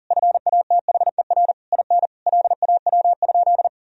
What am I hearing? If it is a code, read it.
Morse code, heard as a keyed tone, about 40 wpm, 700 Hz. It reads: FATHER IN LAW?
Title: Sentences from Top 1000 Words - Review Part 4 - 40wpm